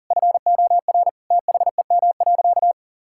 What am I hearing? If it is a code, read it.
Morse code, heard as a keyed tone, about 40 wpm, 700 Hz. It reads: FOR THEM.
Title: Sentences from Top 1000 Words - Review Part 5 - 40wpm